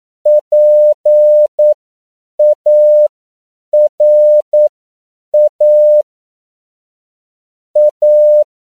Morse code, heard as a keyed tone, about 9 wpm, spaced as for 7 wpm, 600 Hz. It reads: PARA A
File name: CW / 10PPM